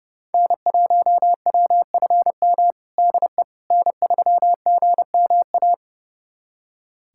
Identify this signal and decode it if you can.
Morse code, heard as a keyed tone, about 32 wpm, 700 Hz. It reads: N1WFM DE N3GMA
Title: USA Callsign DE Callsign - 5 Characters - 30wpm